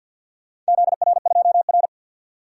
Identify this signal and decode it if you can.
Morse code, heard as a keyed tone, about 49 wpm, 700 Hz. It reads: ZR2R
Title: DXCC-100 Call Signs - 4 Characters - 50wpm